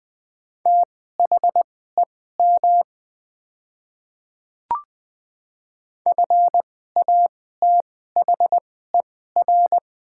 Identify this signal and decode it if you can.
Morse code, heard as a keyed tone, about 21 wpm, spaced as for 15 wpm, 700 Hz. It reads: THEM FATHER
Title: 2x Character Spacing - Top 200 Words - 20wpm